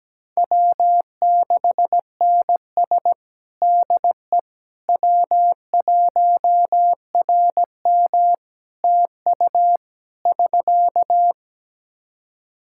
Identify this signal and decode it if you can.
Morse code, heard as a keyed tone, about 17 wpm, 700 Hz. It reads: W6NS DE W1RM TU <SK>
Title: Practice QSO - Exchange 4 - Ending - 17wpm